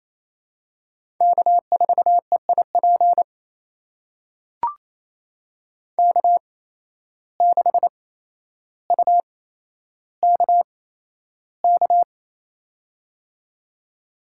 Morse code, heard as a keyed tone, about 27 wpm, spaced as for 12 wpm, 700 Hz. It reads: K4EIP K6UKK
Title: USA Call Signs - 5 Characters - 8x Character Spacing - 28wpm